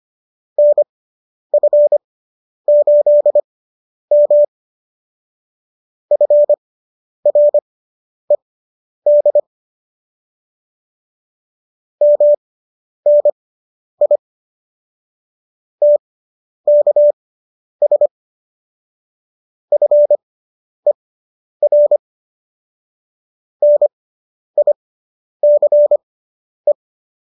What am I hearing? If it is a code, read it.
Morse code, heard as a keyed tone, about 25 wpm, spaced as for 10 wpm, 590 Hz. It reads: NF8M FRED MNI TKS FER NICE